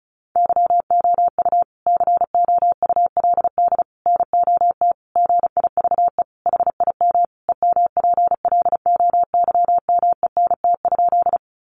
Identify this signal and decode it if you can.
Morse code, heard as a keyed tone, about 34 wpm, 700 Hz. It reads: YOU COULD NOT GIVE HIM EMPLOYMENT?
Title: Sentences from Top 1500 Words 35wpm